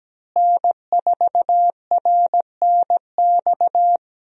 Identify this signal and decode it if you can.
Morse code, heard as a keyed tone, about 17 wpm, 700 Hz. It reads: N4RNX
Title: USA Call Signs - 5 Characters - Repeated 2x - 17wpm